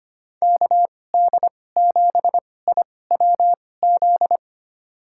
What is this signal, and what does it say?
Morse code, heard as a keyed tone, about 25 wpm, spaced as for 18 wpm, 700 Hz. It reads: KD7IWZ
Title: USA Call Signs - 6 Characters - 2x Character Spacing - 25wpm